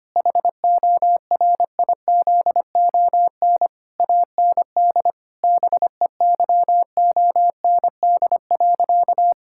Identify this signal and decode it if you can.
Morse code, heard as a keyed tone, about 25 wpm, 700 Hz. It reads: HORIZON AND BEYOND.
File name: practice / 25wpm / Difficult Sentences with Hints - 25wpm